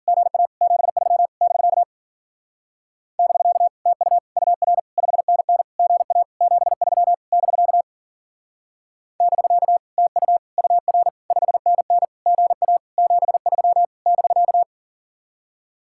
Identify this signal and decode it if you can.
Morse code, heard as a keyed tone, about 52 wpm, 700 Hz. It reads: GA 73 <BK> <BK> TU UR 5NN GA 73 <BK> <BK> TU UR 5NN GA 73 <BK>